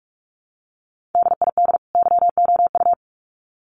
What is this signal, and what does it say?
Morse code, heard as a keyed tone, about 44 wpm, 700 Hz. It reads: DID YOU
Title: Sets of 2 Words 45wpm